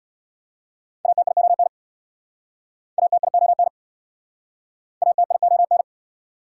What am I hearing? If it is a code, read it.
Morse code, heard as a keyed tone, about 67 wpm, 690 Hz. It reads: ATION ATION ATION